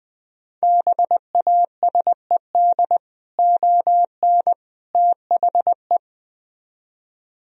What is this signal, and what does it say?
Morse code, heard as a keyed tone, about 20 wpm, 700 Hz. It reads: BASED ON THE